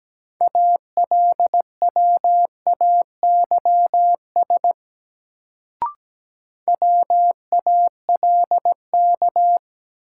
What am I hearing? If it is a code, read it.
Morse code, heard as a keyed tone, about 17 wpm, 700 Hz. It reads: ALWAYS WALK
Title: Sentences from Top 300 Words - New Word Drill - 17wpm